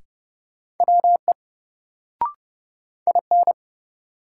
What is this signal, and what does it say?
Morse code, heard as a keyed tone, about 30 wpm, 700 Hz. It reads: WE IN
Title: Sentences from Top 100 Words - New Word Drill - 30wpm